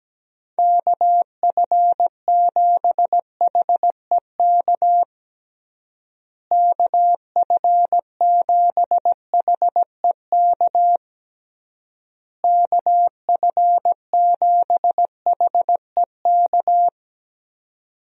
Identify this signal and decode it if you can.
Morse code, heard as a keyed tone, about 17 wpm, 700 Hz. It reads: KF7HEK KF7HEK KF7HEK
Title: USA Call Signs - 6 Characters - Repeated 3x - 17wpm